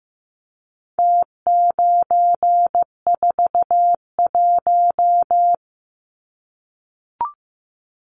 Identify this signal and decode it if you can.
Morse code, heard as a keyed tone, about 15 wpm, 700 Hz. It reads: T941
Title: Numbers - Contest Style - 4-Digit 15wpm